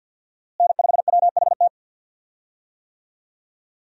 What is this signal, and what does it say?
Morse code, heard as a keyed tone, about 51 wpm, 700 Hz. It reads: N5WHT
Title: USA Call Signs - 5 Characters - Repeated 2x - 50wpm